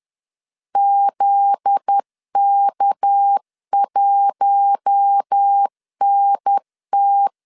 Morse code, heard as a keyed tone, about 10 wpm, 800 Hz. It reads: ZK1NT
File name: LEZIONI / 7 GF9Z / 7 GF9Z 10wpm 2611